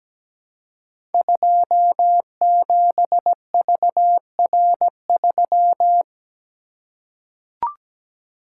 Morse code, heard as a keyed tone, about 17 wpm, 700 Hz. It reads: 27VR3
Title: Code Groups - 5 Letter:Numbers - 17wpm